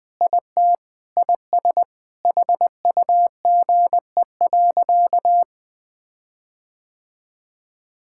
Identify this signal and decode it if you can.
Morse code, heard as a keyed tone, about 20 wpm, 700 Hz. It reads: IT IS HUGE.
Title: Sentences with 6-Word Phrases - Spoken Beginning - 20wpm